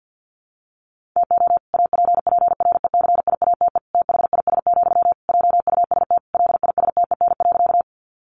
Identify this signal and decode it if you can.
Morse code, heard as a keyed tone, about 50 wpm, 700 Hz. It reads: TO APPRECIATE THIS, JUST LISTEN.
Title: Sentences from Top 1300 Words 50wpm